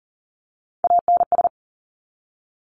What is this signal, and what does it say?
Morse code, heard as a keyed tone, about 38 wpm, 700 Hz. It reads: ANS
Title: Top Three Letter Combinations in Words 40wpm